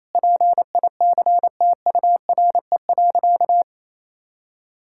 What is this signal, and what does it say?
Morse code, heard as a keyed tone, about 28 wpm, 700 Hz. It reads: PICTURE.
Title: Sentences from Top 1000 Words - Review Part 6 - 28wpm